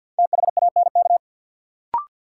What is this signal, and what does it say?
Morse code, heard as a keyed tone, about 53 wpm, 700 Hz. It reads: THANK
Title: Sets of 2 Words - New Word Drill - 50wpm